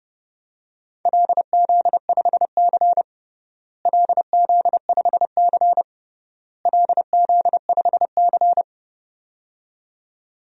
Morse code, heard as a keyed tone, about 32 wpm, 700 Hz. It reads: LZ5C LZ5C LZ5C
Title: DXCC-100 Call Signs - 4 Characters - Repeated 3x - 30wpm